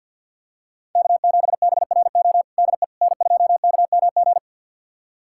Morse code, heard as a keyed tone, about 47 wpm, 700 Hz. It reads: K7BRO DE N2KMG
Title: USA Callsign DE Callsign - 5 Characters - 50wpm